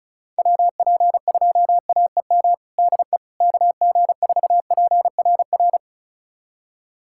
Morse code, heard as a keyed tone, about 36 wpm, 700 Hz. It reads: WP2AEM DE KG4PRR
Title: USA Callsign DE Callsign - 6 Characters - 35wpm